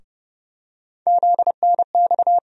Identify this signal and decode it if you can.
Morse code, heard as a keyed tone, about 31 wpm, 700 Hz. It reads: ZNX
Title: USA Call Sign Postfixes - 3 Letters 30wpm